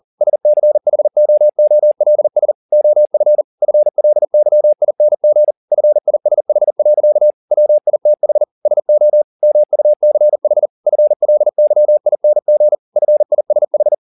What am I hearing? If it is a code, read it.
Morse code, heard as a keyed tone, about 40 wpm, 600 Hz. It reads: SCHOOLS OF FLYING FISH. WITH SO MUCH FLYING FISH
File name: the old man and the sea 2 40wpm